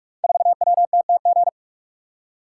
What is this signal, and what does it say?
Morse code, heard as a keyed tone, about 44 wpm, 700 Hz. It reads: 4WTTG
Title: Code Groups - 5 Letter:Numbers - 45wpm